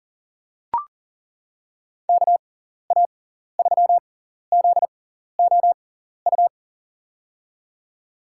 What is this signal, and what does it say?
Morse code, heard as a keyed tone, about 40 wpm, spaced as for 14 wpm, 700 Hz. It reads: KA3ZOU
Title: USA Call Signs - 6 Characters - 6x Character Spacing - 40wpm